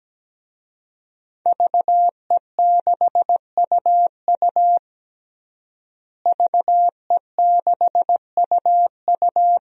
Morse code, heard as a keyed tone, about 17 wpm, 700 Hz. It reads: VE6UU VE6UU
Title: DXCC-100 Call Signs - 5 Characters - Repeated 2x - 17wpm